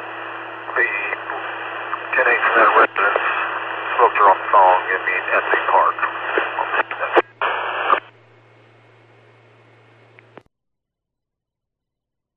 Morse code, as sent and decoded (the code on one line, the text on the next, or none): none